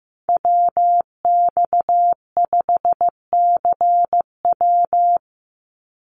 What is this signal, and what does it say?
Morse code, heard as a keyed tone, about 15 wpm, 700 Hz. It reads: WX5CW
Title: Call Signs Repeated 3x - 15wpm